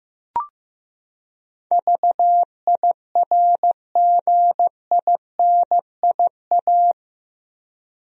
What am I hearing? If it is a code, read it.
Morse code, heard as a keyed tone, about 15 wpm, 700 Hz. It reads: VIRGINIA
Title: USA Common Names - 15wpm